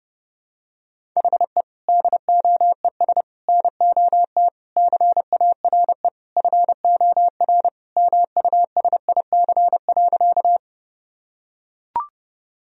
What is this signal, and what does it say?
Morse code, heard as a keyed tone, about 30 wpm, 700 Hz. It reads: HE DOES NOT CARE FOR MUSIC.
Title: Sentences from Top 1000 Words - Review Part 1 - 30wpm